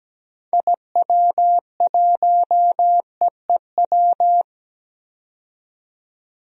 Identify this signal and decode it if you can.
Morse code, heard as a keyed tone, about 17 wpm, 700 Hz. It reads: IW1EEW
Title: DXCC-100 Call Signs - 6 Characters - 17wpm